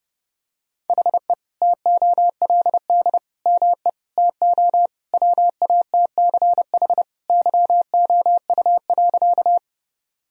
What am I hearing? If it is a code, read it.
Morse code, heard as a keyed tone, about 30 wpm, 700 Hz. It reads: HE TOLD ME TO WATCH YOU.
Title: Sentences from Top 1000 Words - Review Part 1 - 30wpm